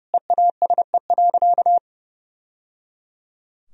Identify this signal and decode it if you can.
Morse code, heard as a keyed tone, about 30 wpm, 700 Hz. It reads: EASE.
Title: Sentences from Top 1000 Words - Review Part 1 - 30wpm